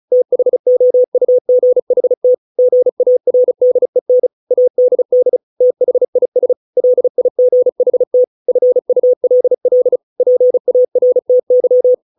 Morse code, heard as a keyed tone, about 34 wpm, 500 Hz. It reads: THOUGHT GARDEN ADD THIS LIGHT FULL PARTY